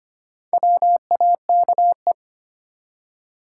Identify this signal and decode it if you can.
Morse code, heard as a keyed tone, about 25 wpm, 700 Hz. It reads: WAKE
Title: Four Letter Words - 25wpm